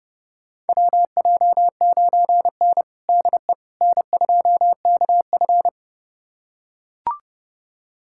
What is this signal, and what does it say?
Morse code, heard as a keyed tone, about 30 wpm, 700 Hz. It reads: WJ9N DE N2KF
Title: USA Callsign DE Callsign - 4 Characters - 30wpm